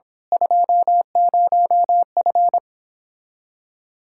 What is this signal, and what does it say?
Morse code, heard as a keyed tone, about 25 wpm, 700 Hz. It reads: 20F
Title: Speed-Racing Common QSO Phrases Encore - 22wpm